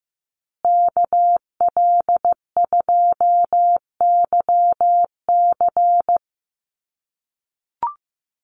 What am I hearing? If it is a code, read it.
Morse code, heard as a keyed tone, about 15 wpm, 700 Hz. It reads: KL2YC
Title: USA Call Signs Repeated 3x 15wpm